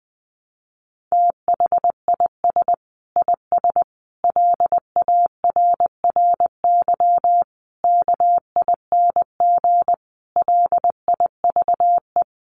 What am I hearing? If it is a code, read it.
Morse code, heard as a keyed tone, about 20 wpm, 700 Hz. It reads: THIS IS LARRY KING LIVE
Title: Sets of 5 Words Encore2 20wpm